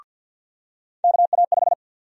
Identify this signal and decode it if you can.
Morse code, heard as a keyed tone, about 50 wpm, 700 Hz. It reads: KA5
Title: USA Call Sign Prefixes - 50wpm